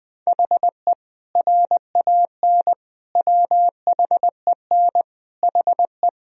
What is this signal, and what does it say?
Morse code, heard as a keyed tone, about 20 wpm, 700 Hz. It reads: HE RAN WHEN HE
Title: Sentences from Top 500 Words and Abbreviations 20wpm